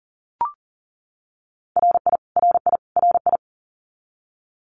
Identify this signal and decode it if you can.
Morse code, heard as a keyed tone, about 40 wpm, 700 Hz. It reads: RI RI RI
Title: US States Repeated 3x 40wpm